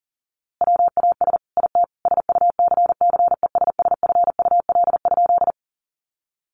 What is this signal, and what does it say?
Morse code, heard as a keyed tone, about 40 wpm, 700 Hz. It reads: WAS IT SUCCESSFUL?
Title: A Sound Of Thunder 40wpm